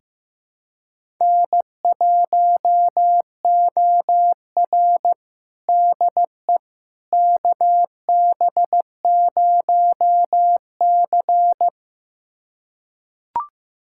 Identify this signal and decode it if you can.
Morse code, heard as a keyed tone, about 15 wpm, 700 Hz. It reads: N1OR DE KB0C
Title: USA Callsign DE Callsign - 4 Characters - 15wpm